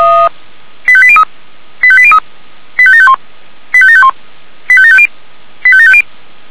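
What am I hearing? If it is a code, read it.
Five-tone selective call (ZVEI), heard as quick runs of five tones, tones of about 70 ms. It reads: 85803 85803 85751 85751 85750 85750
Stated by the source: ZVEI five-tone selective call received by SDR.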